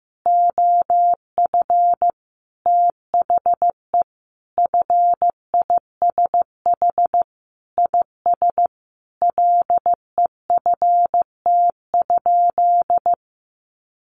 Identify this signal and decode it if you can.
Morse code, heard as a keyed tone, about 15 wpm, 700 Hz. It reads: OF THE FISH IS LEFT?
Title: Sentences from Top 1300 Words 15wpm